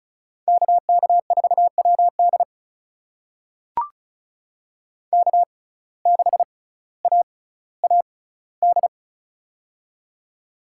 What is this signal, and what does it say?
Morse code, heard as a keyed tone, about 36 wpm, spaced as for 18 wpm, 700 Hz. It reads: KK4WD K6AAD
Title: USA Call Signs - 5 Characters - 6x Character Spacing - 35wpm